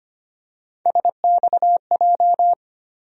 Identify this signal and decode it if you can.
Morse code, heard as a keyed tone, about 25 wpm, 700 Hz. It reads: SXJ
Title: USA Call Sign Postfixes - 3 Letters 25wpm